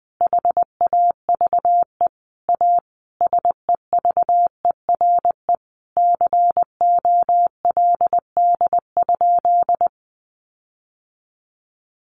Morse code, heard as a keyed tone, about 20 wpm, 700 Hz. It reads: HAVE A SEVERE COLD?
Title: Sentences from Top 1400 Words 20wpm